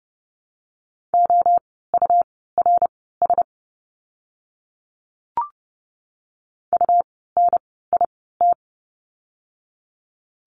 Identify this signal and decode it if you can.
Morse code, heard as a keyed tone, about 30 wpm, spaced as for 17 wpm, 700 Hz. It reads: OURS UNIT
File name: Lesson 17f - Letter u - Words 30wpm